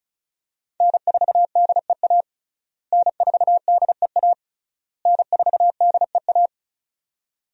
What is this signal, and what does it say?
Morse code, heard as a keyed tone, about 34 wpm, 700 Hz. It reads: N4DEA N4DEA N4DEA
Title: USA Call Signs - 5 Characters - Repeated 3x - 35wpm